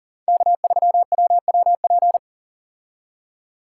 Morse code, heard as a keyed tone, about 40 wpm, 700 Hz. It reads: K3WWP
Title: K1USN SST QSO Practice - 40wpm